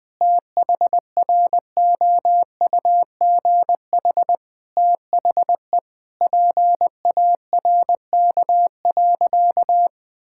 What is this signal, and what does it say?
Morse code, heard as a keyed tone, about 20 wpm, 700 Hz. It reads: THROUGH THE PARK.
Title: Sentences with 2-Word Phrases - 20wpm